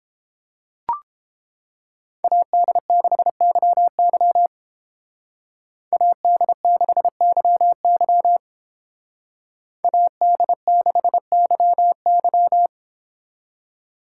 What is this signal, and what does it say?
Morse code, heard as a keyed tone, about 28 wpm, 700 Hz. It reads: AD6YY AD6YY AD6YY